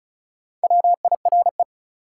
Morse code, heard as a keyed tone, about 36 wpm, 700 Hz. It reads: WIRE
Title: Top 180 QSO Elements 35wpm